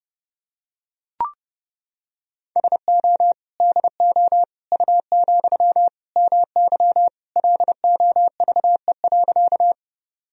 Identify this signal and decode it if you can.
Morse code, heard as a keyed tone, about 32 wpm, 700 Hz. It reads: SO DO U, MY LOVE.
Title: Sentences from Top 400 Words and Abbreviations 30wpm